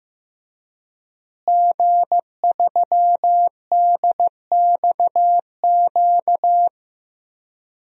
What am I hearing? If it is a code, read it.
Morse code, heard as a keyed tone, about 16 wpm, 700 Hz. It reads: G3DXQ
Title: DXCC-100 Call Signs - 5 Characters - Repeated 3x - 15wpm